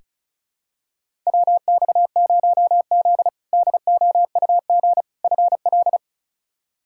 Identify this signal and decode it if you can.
Morse code, heard as a keyed tone, about 36 wpm, 700 Hz. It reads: WX0Z DOUG FL
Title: K1USN SST QSO Practice - 35wpm